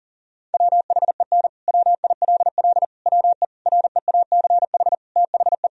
Morse code, heard as a keyed tone, about 40 wpm, 700 Hz. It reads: WHEN WILL WE REACH THE
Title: Sentences from Top 1000 Words - Review Part 4 - 40wpm